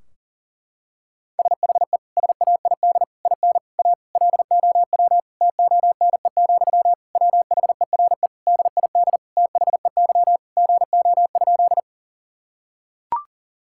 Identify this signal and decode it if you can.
Morse code, heard as a keyed tone, about 40 wpm, 700 Hz. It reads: SHE SAID IN A LOW TONE, WHERE DID THEY GO?
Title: Sentences from Top 1600 Words 40wpm